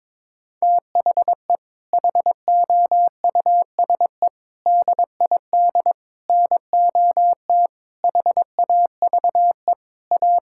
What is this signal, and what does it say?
Morse code, heard as a keyed tone, about 22 wpm, 700 Hz. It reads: THE HOUSE DID NOT HAVE A
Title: Sentences from Top 1600 Words 22wpm